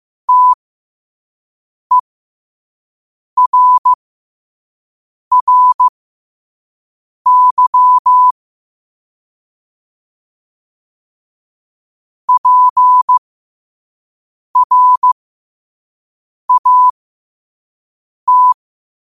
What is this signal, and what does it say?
Morse code, heard as a keyed tone, about 15 wpm, spaced as for 5 wpm, 1000 Hz. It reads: TERRY PRAT
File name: qotdfiles / 5wpm